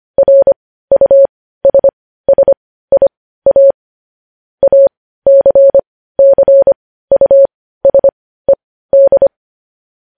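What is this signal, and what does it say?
Morse code, heard as a keyed tone, about 25 wpm, spaced as for 15 wpm, 550 Hz. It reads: RUSSIA ACCUSED
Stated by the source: Morse code news headlines sent at 15, opening with vvv.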